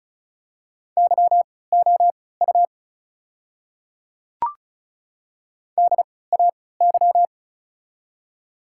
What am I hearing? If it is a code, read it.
Morse code, heard as a keyed tone, about 36 wpm, spaced as for 20 wpm, 690 Hz. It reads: YOU DAY
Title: 3x Character Spacing - Top 100 Words - 35wpm